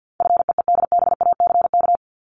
Morse code, heard as a keyed tone, about 50 wpm, 700 Hz. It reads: FEEDBACK
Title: Top 1501-1750 Words 50wpm